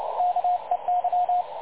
Morse code, heard as a keyed tone, about 29 wpm, 720 Hz, with radio noise. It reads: KEY
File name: dlazaawansowanych / nagraniazpasma / websdr recording 3565.0kHz